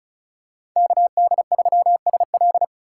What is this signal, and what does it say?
Morse code, heard as a keyed tone, about 34 wpm, 700 Hz. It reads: KD3SL
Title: USA Call Signs - 3x Character Spacing - 35wpm